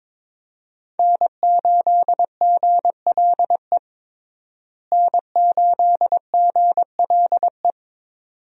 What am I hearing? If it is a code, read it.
Morse code, heard as a keyed tone, about 22 wpm, 700 Hz. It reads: N8GLE N8GLE
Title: USA Call Signs - 5 Characters - Repeated 2x - 22wpm